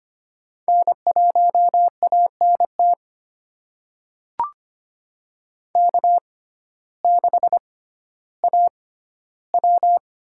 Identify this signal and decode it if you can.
Morse code, heard as a keyed tone, about 25 wpm, spaced as for 14 wpm, 700 Hz. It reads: N1ANT K6AW
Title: USA Call Signs - 6x Character Spacing - 25wpm